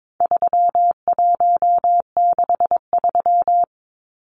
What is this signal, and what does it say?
Morse code, heard as a keyed tone, about 22 wpm, 700 Hz. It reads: 3163
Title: Numbers - 4-Digit 22wpm